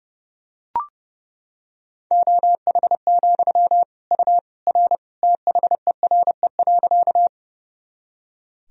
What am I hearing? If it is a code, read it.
Morse code, heard as a keyed tone, about 30 wpm, 700 Hz. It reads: OH, U R THERE.
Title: Sentences from Top 500 Words and Abbreviations 30wpm